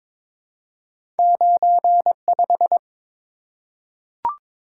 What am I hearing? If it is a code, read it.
Morse code, heard as a keyed tone, about 22 wpm, 690 Hz. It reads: 95
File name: practice / 22wpm / Two Letter-Number Combinations - 22wpm